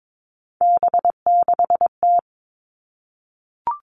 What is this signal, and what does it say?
Morse code, heard as a keyed tone, about 22 wpm, 700 Hz. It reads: B6T
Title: Three Challenging Letter-Number Combinations 22wpm